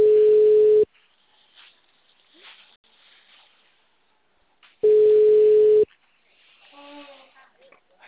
Ringback tone (Europe, 425 Hz).
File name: out- -2045- - - .61814